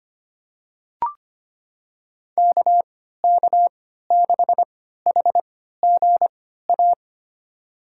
Morse code, heard as a keyed tone, about 25 wpm, spaced as for 14 wpm, 700 Hz. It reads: KK6HGA
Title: USA Call Signs - 3x Character Spacing - 25wpm